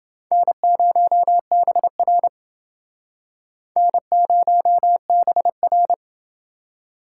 Morse code, heard as a keyed tone, about 29 wpm, 700 Hz. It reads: N0BR N0BR